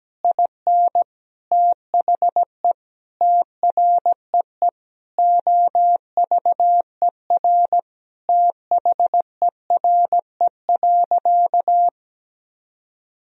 Morse code, heard as a keyed tone, about 17 wpm, 700 Hz. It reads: IN THE TREE OVER THERE.
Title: Sentences from Top 1000 Words - Review Part 1 - 17wpm